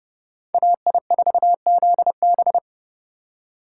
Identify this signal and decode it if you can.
Morse code, heard as a keyed tone, about 29 wpm, 700 Hz. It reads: AI4ZB